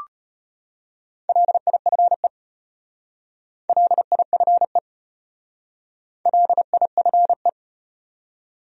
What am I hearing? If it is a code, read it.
Morse code, heard as a keyed tone, about 33 wpm, 700 Hz. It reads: LIFE LIFE LIFE